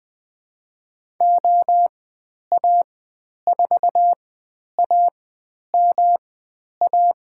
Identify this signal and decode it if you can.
Morse code, heard as a keyed tone, about 20 wpm, spaced as for 10 wpm, 700 Hz. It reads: OA4AMA
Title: 109 Lesson 18g - Letter m - Call Signs 20wpm